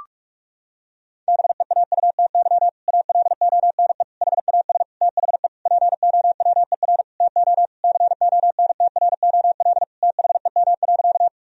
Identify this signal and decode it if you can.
Morse code, heard as a keyed tone, about 46 wpm, 700 Hz. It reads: BEAUTY ALONE HAS THE POWER TO CONTROL THEM.
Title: Sentences from Top 1000 Words - Review Part 2 - 45wpm